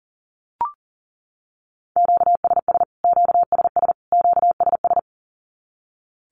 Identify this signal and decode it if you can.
Morse code, heard as a keyed tone, about 40 wpm, 700 Hz. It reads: QSS QSS QSS
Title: Q-Codes Repeated 3x 40wpm